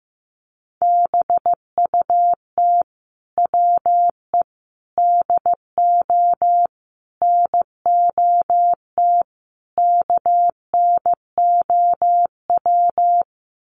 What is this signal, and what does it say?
Morse code, heard as a keyed tone, about 15 wpm, 700 Hz. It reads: BUT WE DO NOT KNOW
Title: Sets of 5 Words Encore2 15wpm